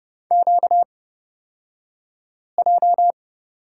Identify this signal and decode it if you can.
Morse code, heard as a keyed tone, about 30 wpm, spaced as for 5 wpm, 700 Hz. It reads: QJ
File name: Lesson 38d - Letter q - Two Characters 30wpm